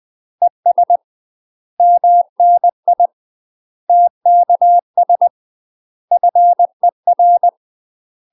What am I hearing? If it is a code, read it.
Morse code, heard as a keyed tone, about 20 wpm, 700 Hz. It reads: ES MNI TKS FER
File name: UK MASTER QSOs TRAINING / 20WPM / UK MASTER QSOs 0010